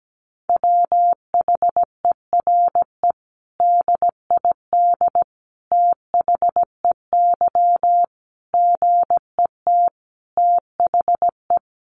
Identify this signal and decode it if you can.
Morse code, heard as a keyed tone, about 17 wpm, 700 Hz. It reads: WHERE DID THEY GET THE
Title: Sentences from Top 400 Words 17wpm